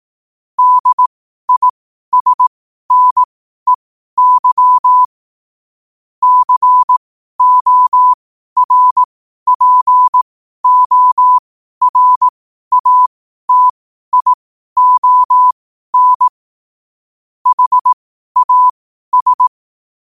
Morse code, heard as a keyed tone, about 18 wpm, spaced as for 12 wpm, 1000 Hz. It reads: DISNEY CORPORATION HAS